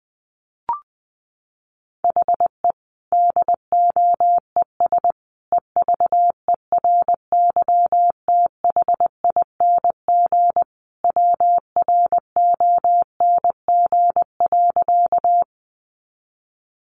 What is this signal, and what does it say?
Morse code, heard as a keyed tone, about 20 wpm, 700 Hz. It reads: HE DOES EVERYTHING WRONG.
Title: Sentences from Top 900 Words 20wpm